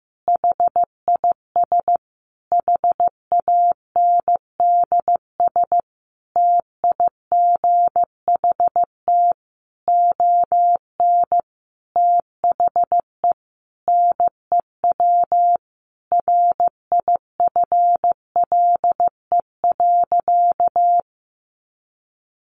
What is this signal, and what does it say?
Morse code, heard as a keyed tone, about 14 wpm, 700 Hz. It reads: HIS HANDS TIGHT ON THE NEW RIFLE.
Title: A Sound Of Thunder 15wpm